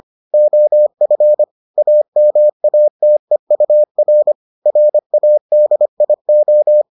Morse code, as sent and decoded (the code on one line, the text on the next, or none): --- ..-. / .- -- .- - . ..- .-. / .-. .- -.. .. ---
OF AMATEUR RADIO